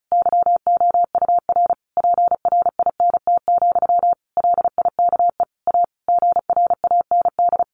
Morse code, heard as a keyed tone, about 34 wpm, 700 Hz. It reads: YOUR PRINT, LIKE A GRAND
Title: A Sound Of Thunder 35wpm